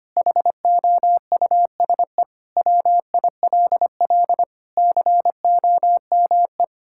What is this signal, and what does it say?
Morse code, heard as a keyed tone, about 25 wpm, 700 Hz. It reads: HOUSE WILL COME
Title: Sentences from Top 1600 Words 25wpm